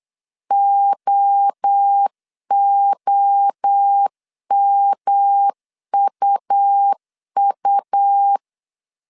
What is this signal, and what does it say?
Morse code, heard as a keyed tone, about 8 wpm, 800 Hz. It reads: OOMUU
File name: CW / DET / 1 3 1500C S B 8wpm 4141